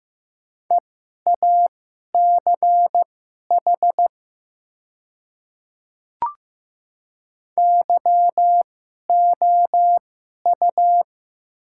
Morse code, heard as a keyed tone, about 15 wpm, spaced as for 11 wpm, 700 Hz. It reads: EACH YOU
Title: 2x Character Spacing - Top 100 Words - 15wpm